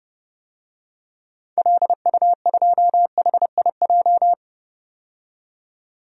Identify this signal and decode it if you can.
Morse code, heard as a keyed tone, about 32 wpm, 700 Hz. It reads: LU2HIJ
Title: DXCC-100 Call Signs - 6 Characters - 30wpm